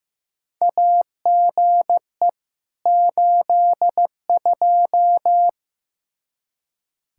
Morse code, heard as a keyed tone, about 15 wpm, 700 Hz. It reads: AGE 82
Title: Practice QSO - Exchange 3 - More Info - 15wpm